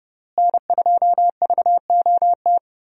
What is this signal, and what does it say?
Morse code, heard as a keyed tone, about 32 wpm, 700 Hz. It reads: N2VOT
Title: USA Call Signs - 5 Characters - Repeated 3x - 30wpm